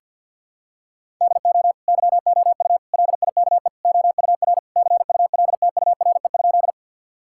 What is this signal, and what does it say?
Morse code, heard as a keyed tone, about 50 wpm, 700 Hz. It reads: DO YOU LIKE OUR CULTURE?
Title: Sentences from Top 1000 Words - Review Part 6 - 50wpm